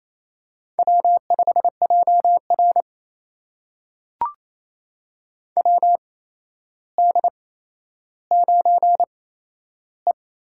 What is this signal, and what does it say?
Morse code, heard as a keyed tone, about 28 wpm, spaced as for 12 wpm, 700 Hz. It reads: W5JR WD9E
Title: USA Call Signs - 4 Characters - 8x Character Spacing - 28wpm